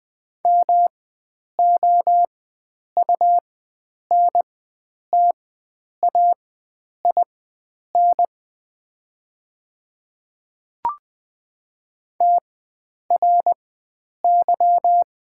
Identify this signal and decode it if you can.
Morse code, heard as a keyed tone, about 20 wpm, spaced as for 9 wpm, 700 Hz. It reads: MOUNTAIN TRY
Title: 4x Character Spacing - Top 300 Words - 20wpm